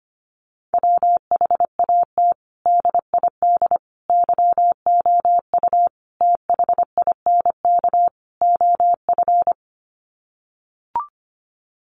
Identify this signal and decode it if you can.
Morse code, heard as a keyed tone, about 25 wpm, 700 Hz. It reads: WHAT DID YOU THINK OF
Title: Sets of 5 Words Encore2 25wpm